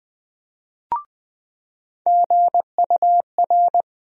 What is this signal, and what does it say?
Morse code, heard as a keyed tone, about 20 wpm, 700 Hz. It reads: GUR
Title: USA Call Sign Postfixes - 3 Letters 20wpm